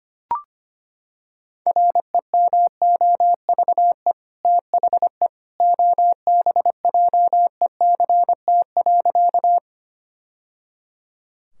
Morse code, heard as a keyed tone, about 25 wpm, 700 Hz. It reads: REMOVE THE OBJECT.
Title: Sentences from Top 1000 Words - Review Part 5 - 25wpm